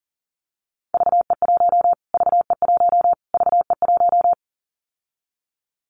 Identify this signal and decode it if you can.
Morse code, heard as a keyed tone, about 37 wpm, 700 Hz. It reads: VE1 VE1 VE1
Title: Callsign Prefixes for Top 25 DXCC - Repeated 3x - 40wpm